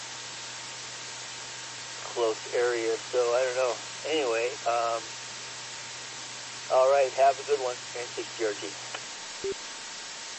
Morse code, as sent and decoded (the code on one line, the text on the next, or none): none